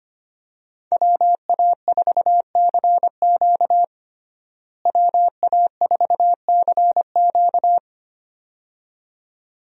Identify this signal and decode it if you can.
Morse code, heard as a keyed tone, about 25 wpm, 700 Hz. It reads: WA4CQ WA4CQ
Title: USA Call Signs - 5 Characters - Repeated 2x - 25wpm